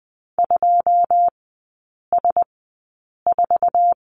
Morse code, heard as a keyed tone, about 20 wpm, spaced as for 8 wpm, 700 Hz. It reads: 2S4
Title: 100 Lesson 17e - Letter u - Three Characters 20wpm